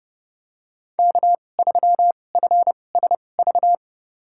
Morse code, heard as a keyed tone, about 32 wpm, spaced as for 22 wpm, 700 Hz. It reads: K3FSV
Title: USA Call Signs - 2x Character Spacing - 30wpm